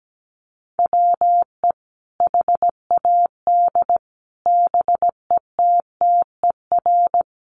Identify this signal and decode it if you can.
Morse code, heard as a keyed tone, about 17 wpm, 700 Hz. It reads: WE HAD BETTER
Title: Sentences from Top 1100 Words 17wpm